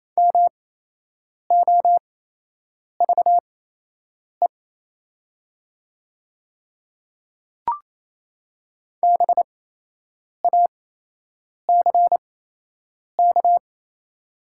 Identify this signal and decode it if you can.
Morse code, heard as a keyed tone, about 28 wpm, spaced as for 8 wpm, 700 Hz. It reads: MOVE BACK
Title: 8x Character Spacing - Top 200 Words - 28wpm